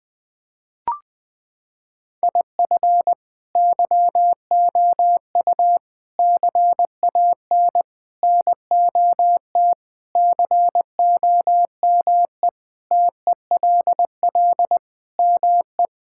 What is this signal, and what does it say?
Morse code, heard as a keyed tone, about 19 wpm, 700 Hz. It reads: IF YOU CAN NOT COME TELL ME
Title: Sentences with 4-Word Phrases - 20wpm